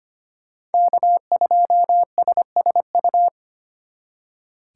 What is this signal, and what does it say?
Morse code, heard as a keyed tone, about 33 wpm, 700 Hz. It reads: K2SSU